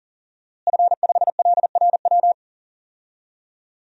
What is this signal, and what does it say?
Morse code, heard as a keyed tone, about 40 wpm, 700 Hz. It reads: F5LRW
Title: DXCC-100 Call Signs - 5 Characters - Repeated 2x - 40wpm